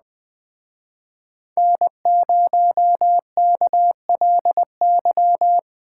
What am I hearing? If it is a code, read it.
Morse code, heard as a keyed tone, about 20 wpm, 700 Hz. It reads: N0KLY